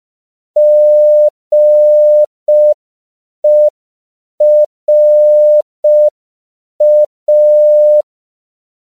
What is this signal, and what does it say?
Morse code, heard as a keyed tone, about 5 wpm, 600 Hz. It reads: GERA
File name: CW / 5PPM